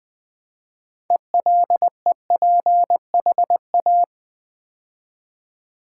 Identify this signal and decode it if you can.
Morse code, heard as a keyed tone, about 21 wpm, 700 Hz. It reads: ELEPHA
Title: Build A Word - 2 Letters At A Time - 8-Letter Words - 20wpm